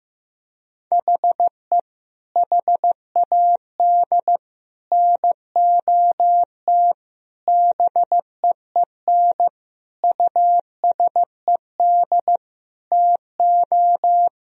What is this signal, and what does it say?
Morse code, heard as a keyed tone, about 15 wpm, 700 Hz. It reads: HE HAD NOT BEEN USED TO
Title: Sentences from Top 1000 Words - Review Part 4 - 15wpm